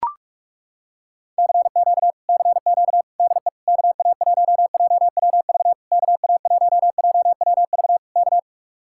Morse code, heard as a keyed tone, about 40 wpm, 700 Hz. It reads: CQ CQ DE KA1JWV KA1JWV K